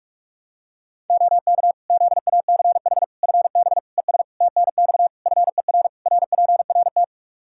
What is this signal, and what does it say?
Morse code, heard as a keyed tone, about 46 wpm, 700 Hz. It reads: OK ZACH FB ES TNX FER RPRT
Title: K1USN SST QSO Practice 45wpm